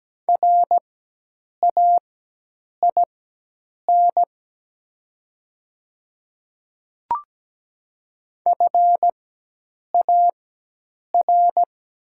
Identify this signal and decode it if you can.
Morse code, heard as a keyed tone, about 17 wpm, spaced as for 8 wpm, 700 Hz. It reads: RAIN FAR
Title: 4x Character Spacing - Top 300 Words - 17wpm